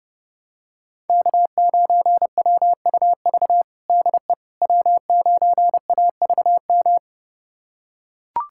Morse code, heard as a keyed tone, about 30 wpm, 700 Hz. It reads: K9WUV DE W9AVM
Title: USA Callsign DE Callsign - 5 Characters - 30wpm